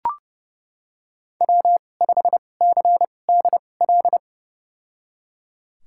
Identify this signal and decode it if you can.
Morse code, heard as a keyed tone, about 30 wpm, spaced as for 22 wpm, 700 Hz. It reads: W5CDL
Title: USA Call Signs - 5 Characters - 2x Character Spacing - 30wpm